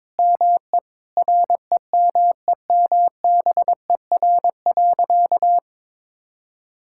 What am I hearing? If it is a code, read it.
Morse code, heard as a keyed tone, about 22 wpm, 700 Hz. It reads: ME REMEMBER.
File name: practice / 22wpm / Sentences with 8-Word Phrases - 22wpm